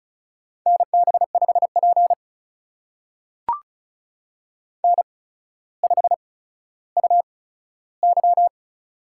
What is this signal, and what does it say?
Morse code, heard as a keyed tone, about 35 wpm, spaced as for 15 wpm, 700 Hz. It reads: NB5P N5UY
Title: USA Call Signs - 4 Characters - 8x Character Spacing - 35wpm